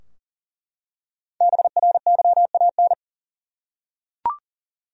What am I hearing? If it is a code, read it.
Morse code, heard as a keyed tone, about 40 wpm, 700 Hz. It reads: BRYAN
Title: USA Common Names - 40wpm